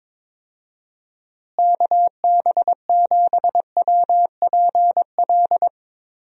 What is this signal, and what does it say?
Morse code, heard as a keyed tone, about 22 wpm, 700 Hz. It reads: KB7WPL
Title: USA Call Signs - 6 Characters - Repeated 2x - 22wpm